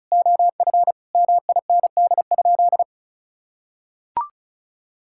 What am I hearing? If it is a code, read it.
Morse code, heard as a keyed tone, about 36 wpm, 700 Hz. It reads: OF MIND?
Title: Sentences from Top 1000 Words - Review Part 6 - 35wpm